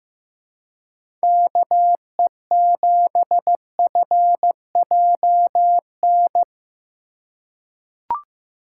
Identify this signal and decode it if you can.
Morse code, heard as a keyed tone, about 15 wpm, 700 Hz. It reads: KE7FJN
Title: USA Call Signs - 6 Characters - 3x Character Spacing - 15wpm